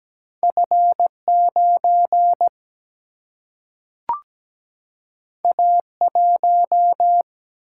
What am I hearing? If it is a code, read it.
Morse code, heard as a keyed tone, about 17 wpm, 700 Hz. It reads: F 9 A 1
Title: Single-Letter Single-Number 17wpm